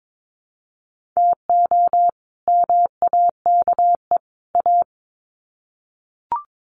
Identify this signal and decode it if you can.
Morse code, heard as a keyed tone, about 22 wpm, 700 Hz. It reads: TO MAKE A
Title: Sets of 3 Words 22wpm